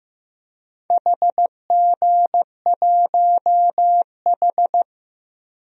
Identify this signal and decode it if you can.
Morse code, heard as a keyed tone, about 15 wpm, 700 Hz. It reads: HG1H
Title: DXCC-100 Call Signs - 4 Characters - 15wpm